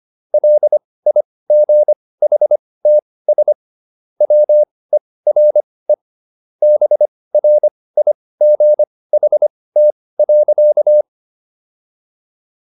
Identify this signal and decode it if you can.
Morse code, heard as a keyed tone, about 25 wpm, spaced as for 18 wpm, 600 Hz. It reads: LIGHTS WERE BRIGHT.